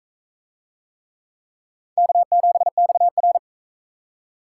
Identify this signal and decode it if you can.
Morse code, heard as a keyed tone, about 42 wpm, 700 Hz. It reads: K7XR